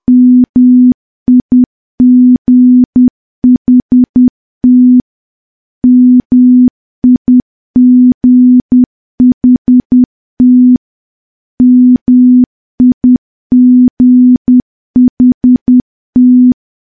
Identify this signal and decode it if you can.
Morse code, heard as a keyed tone, about 10 wpm, 250 Hz. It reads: MIGHT MIGHT MIGHT
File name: morse / morse-training-10WPM